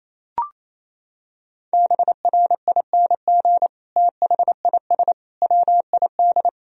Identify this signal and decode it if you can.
Morse code, heard as a keyed tone, about 27 wpm, 700 Hz. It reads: BRING THIS WID
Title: Sentences from Top 500 Words and Abbreviations 28wpm